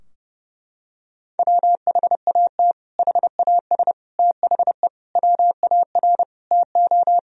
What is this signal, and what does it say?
Morse code, heard as a keyed tone, about 30 wpm, 700 Hz. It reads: WHAT HAS THE WAR TO
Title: Sentences from Top 1000 Words - Review Part 2 - 30wpm